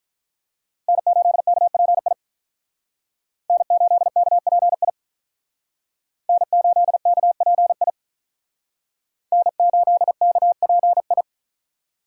Morse code, heard as a keyed tone, about 48 wpm, 700 Hz. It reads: N8KPI N8KPI N8KPI N8KPI